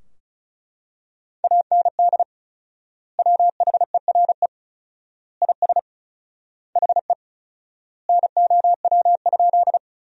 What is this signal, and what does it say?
Morse code, heard as a keyed tone, about 36 wpm, 700 Hz. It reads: AND WHERE IS HE NOW?
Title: 4x Word-Spacing - Sentences from Top 200 Words - 35wpm